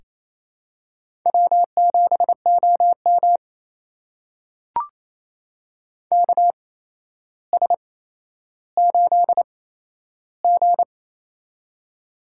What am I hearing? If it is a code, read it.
Morse code, heard as a keyed tone, about 28 wpm, spaced as for 12 wpm, 700 Hz. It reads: W7OM KS8G
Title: USA Call Signs - 4 Characters - 8x Character Spacing - 28wpm